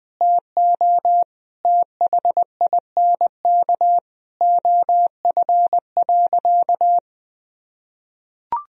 Morse code, heard as a keyed tone, about 20 wpm, 700 Hz. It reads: TO THINK OF.
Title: Sentences from Top 1000 Words - Review Part 1 - 20wpm